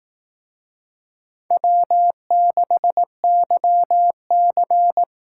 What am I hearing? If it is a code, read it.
Morse code, heard as a keyed tone, about 19 wpm, 700 Hz. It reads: W6YC